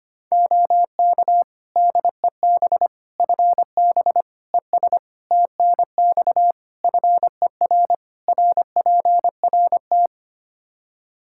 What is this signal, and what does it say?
Morse code, heard as a keyed tone, about 25 wpm, 700 Hz. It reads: OK DEB FB ES TNX FER RPRT
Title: Practice QSO - Exchange 2 - Info - 25wpm